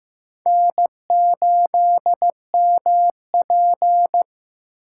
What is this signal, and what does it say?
Morse code, heard as a keyed tone, about 15 wpm, 700 Hz. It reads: N8MP
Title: USA Call Signs - 4 Characters - 6x Character Spacing - 15wpm